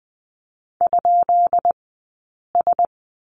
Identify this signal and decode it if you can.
Morse code, heard as a keyed tone, about 20 wpm, 700 Hz. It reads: ? S
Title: 165 Lesson 27d - Number 7 - Two Characters 20wpm